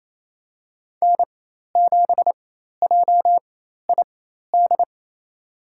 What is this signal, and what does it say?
Morse code, heard as a keyed tone, about 28 wpm, spaced as for 13 wpm, 700 Hz. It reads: N7JID
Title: USA Call Signs - 5 Characters - 4x Character Spacing - 28wpm